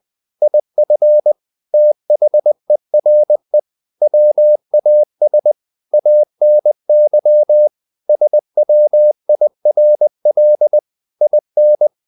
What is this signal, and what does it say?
Morse code, heard as a keyed tone, about 20 wpm, 600 Hz. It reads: IF THERE WAS ANY SWIRL IN
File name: the old man and the sea part 2 20wpm